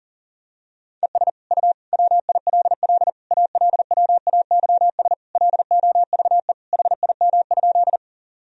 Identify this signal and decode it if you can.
Morse code, heard as a keyed tone, about 40 wpm, 700 Hz. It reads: ES U WILL ALWAYS LOVE HIM?
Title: Sentences from Top 400 Words and Abbreviations 40wpm